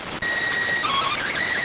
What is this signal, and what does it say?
SSTV (slow-scan television), PD 290.